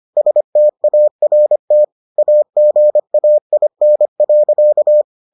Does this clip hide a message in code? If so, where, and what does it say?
Yes: Morse code from 0.2 to 5.0 s, heard as a keyed tone, about 25 wpm, 600 Hz. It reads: START AGAIN.